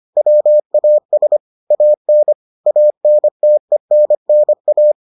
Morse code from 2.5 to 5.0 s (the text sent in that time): ANTENNA